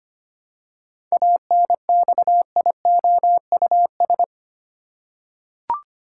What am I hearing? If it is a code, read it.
Morse code, heard as a keyed tone, about 25 wpm, 700 Hz. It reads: ANXIOUS
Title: Seven Letter Words - 25wpm